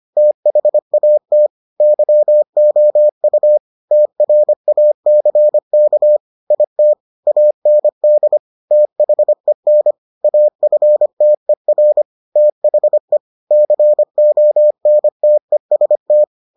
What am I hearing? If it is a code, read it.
Morse code, heard as a keyed tone, about 25 wpm, 600 Hz. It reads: THAT YOU TRACK IT AND THEN AFTER THE CONTEST